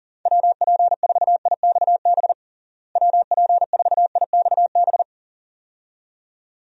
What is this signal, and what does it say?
Morse code, heard as a keyed tone, about 40 wpm, 700 Hz. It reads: WP4IXB WP4IXB
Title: USA Call Signs - 6 Characters - Repeated 2x - 40wpm